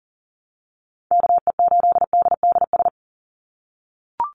Morse code, heard as a keyed tone, about 40 wpm, 700 Hz. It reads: KE8DDS
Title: USA Call Signs 40wpm